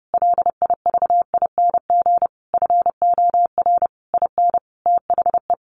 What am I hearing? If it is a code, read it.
Morse code, heard as a keyed tone, about 30 wpm, 700 Hz. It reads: LIVING FOR IN THE
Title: Sentences from Top 1100 Words 30wpm